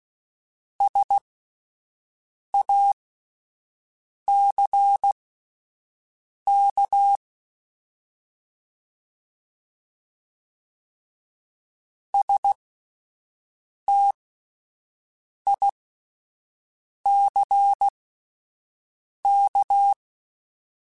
Morse code, heard as a keyed tone, about 16 wpm, spaced as for 5 wpm, 800 Hz. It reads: SACK STICK